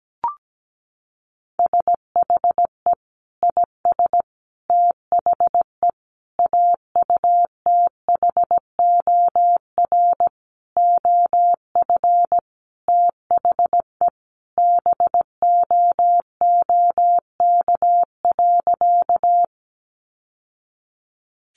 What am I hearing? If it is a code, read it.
Morse code, heard as a keyed tone, about 17 wpm, 700 Hz. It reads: SHE IS THE AUTHOR OF THE BOOK.
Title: Sentences from Top 1200 Words 17wpm